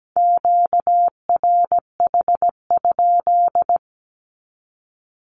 Morse code, heard as a keyed tone, about 17 wpm, 700 Hz. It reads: QRH?
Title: Q-Codes 17wpm